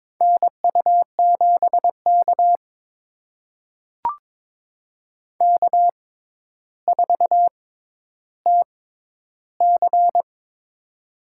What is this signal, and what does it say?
Morse code, heard as a keyed tone, about 22 wpm, spaced as for 11 wpm, 700 Hz. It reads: NU7K K4TC
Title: USA Call Signs - 4 Characters - 6x Character Spacing - 22wpm